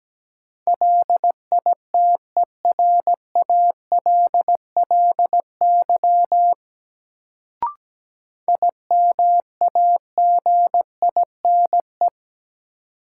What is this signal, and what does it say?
Morse code, heard as a keyed tone, about 17 wpm, 700 Hz. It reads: LITERALLY IMAGINE
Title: Sentences from Top 1400 Words - New Word Drill - 17wpm